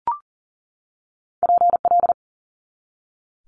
Morse code, heard as a keyed tone, about 41 wpm, 690 Hz. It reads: PL
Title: Multi-letter Phonemes 40wpm